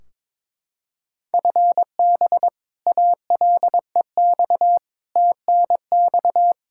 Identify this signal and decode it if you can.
Morse code, heard as a keyed tone, about 21 wpm, 700 Hz. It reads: FB ALEX TNX
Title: Practice QSO - Exchange 3 - More Info - 22wpm